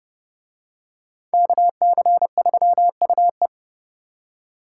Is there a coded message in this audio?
Yes: Morse code from 1.3 to 3.5 s, heard as a keyed tone, about 27 wpm, 700 Hz. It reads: KC3UE